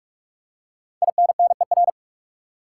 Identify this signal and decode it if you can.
Morse code, heard as a keyed tone, about 47 wpm, 700 Hz. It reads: INNER
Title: Five Letter Words - 45wpm